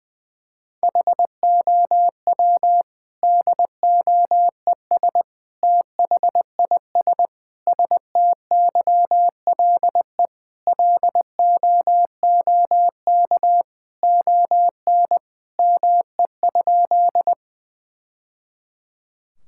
Morse code, heard as a keyed tone, about 21 wpm, 700 Hz. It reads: HOW DOES THIS STYLE LOOK ON ME?
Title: Sentences from Top 1000 Words - Review Part 5 - 20wpm